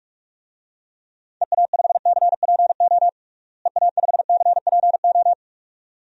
Morse code, heard as a keyed tone, about 46 wpm, 700 Hz. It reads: EA5CPO EA5CPO
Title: DXCC-100 Call Signs - 6 Characters - Repeated 2x - 45wpm